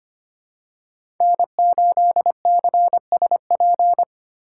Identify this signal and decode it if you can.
Morse code, heard as a keyed tone, about 25 wpm, 700 Hz. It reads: N8CSP
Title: USA Call Signs - 5 Characters - 4x Character Spacing - 25wpm